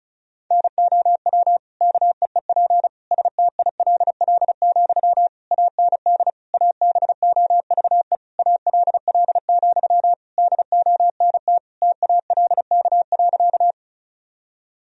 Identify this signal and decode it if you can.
Morse code, heard as a keyed tone, about 36 wpm, 700 Hz. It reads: NOW KEEP STILL, AND ABOVE ALL, DONT TALK.
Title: Sentences from Top 1000 Words - Review Part 2 - 35wpm